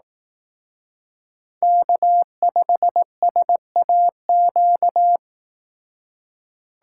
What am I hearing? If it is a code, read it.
Morse code, heard as a keyed tone, about 18 wpm, 700 Hz. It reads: K5SAQ
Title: Speed Racing - USA 1x3 Call Signs 15wpm